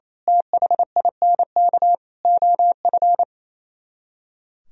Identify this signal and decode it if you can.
Morse code, heard as a keyed tone, about 27 wpm, 700 Hz. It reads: THINK OF
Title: Speed-Racing - Sets of 5 Words 28wpm